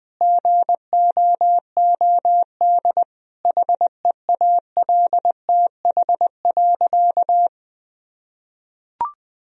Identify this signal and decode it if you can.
Morse code, heard as a keyed tone, about 20 wpm, 700 Hz. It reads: GOOD HEALTH.
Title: Sentences from Top 1000 Words - Review Part 3 - 20wpm